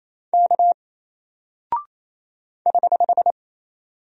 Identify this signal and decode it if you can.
Morse code, heard as a keyed tone, about 28 wpm, 700 Hz. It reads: K <HH>
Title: Single Character (Letter:Number:Punctuation:Prosign) Alt 28wpm